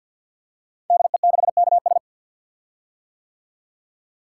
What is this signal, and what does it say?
Morse code, heard as a keyed tone, about 50 wpm, 700 Hz. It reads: DE6KS
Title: Code Groups - 5 Letter:Numbers - 50wpm